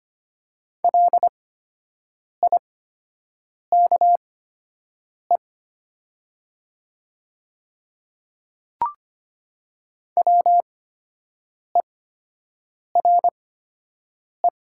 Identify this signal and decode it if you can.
Morse code, heard as a keyed tone, about 25 wpm, spaced as for 7 wpm, 700 Hz. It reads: LIKE WERE
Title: 8x Character Spacing - Top 200 Words - 25wpm